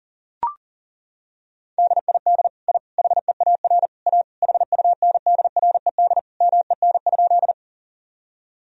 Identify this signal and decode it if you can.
Morse code, heard as a keyed tone, about 37 wpm, 700 Hz. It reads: DID I HEAR A HUNDRED MEN?
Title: Sentences from Top 1000 Words - Review Part 2 - 40wpm